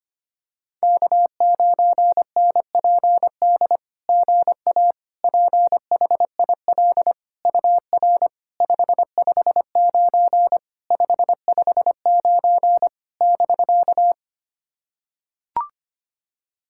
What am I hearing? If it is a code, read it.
Morse code, heard as a keyed tone, about 25 wpm, 660 Hz. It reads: K9NPD GA PHIL UR 559 559 <BK>
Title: SOTA QSO Practice 25wpm